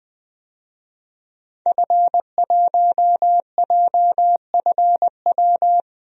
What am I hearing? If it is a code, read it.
Morse code, heard as a keyed tone, about 20 wpm, 700 Hz. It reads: F1JFW
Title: DXCC-100 Call Signs - 5 Characters - 20wpm